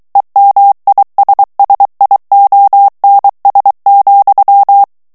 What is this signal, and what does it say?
Morse code, heard as a keyed tone, about 23 wpm, 800 Hz. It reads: EMISSIONS,
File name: ANGLAIS texte clair 3 vitesse 22 mots minute